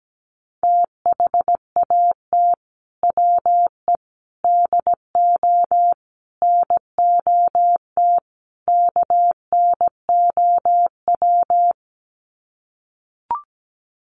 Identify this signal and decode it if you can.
Morse code, heard as a keyed tone, about 17 wpm, 700 Hz. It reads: THAT WE DO NOT KNOW
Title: Sets of 5 Words Encore2 17wpm